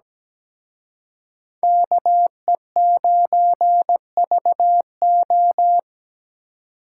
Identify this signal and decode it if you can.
Morse code, heard as a keyed tone, about 20 wpm, 700 Hz. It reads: KE9VO
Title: Speed Racing - USA 2x2 Call Signs 17wpm